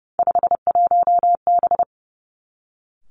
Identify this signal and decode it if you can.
Morse code, heard as a keyed tone, about 30 wpm, 700 Hz. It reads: 51B
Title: Three Challenging Letter-Number Combinations 30wpm